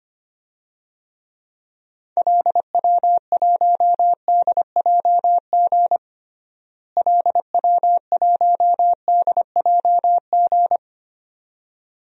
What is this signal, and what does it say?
Morse code, heard as a keyed tone, about 25 wpm, 700 Hz. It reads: LW1DJG LW1DJG
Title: DXCC-100 Call Signs - 6 Characters - Repeated 2x - 25wpm